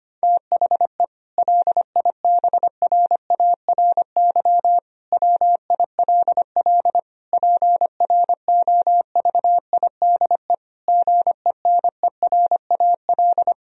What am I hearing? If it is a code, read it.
Morse code, heard as a keyed tone, about 25 wpm, 700 Hz. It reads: THE LIBRARY WILL PROVIDE GENERAL
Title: Sentences from Top 1000 Words - Review Part 4 - 25wpm